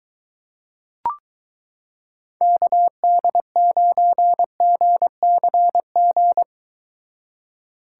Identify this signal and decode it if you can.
Morse code, heard as a keyed tone, about 23 wpm, 700 Hz. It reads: KD9GCG